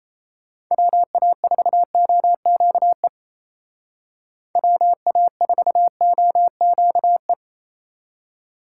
Morse code, heard as a keyed tone, about 34 wpm, 700 Hz. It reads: WA4OQE WA4OQE
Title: Speed Racing - USA Call Signs 28wpm